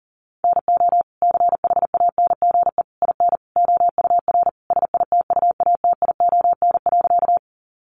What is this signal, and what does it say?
Morse code, heard as a keyed tone, about 40 wpm, 700 Hz. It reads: NO CHANGE IN OUR SITUATION.
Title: Sentences from Top 1200 Words 40wpm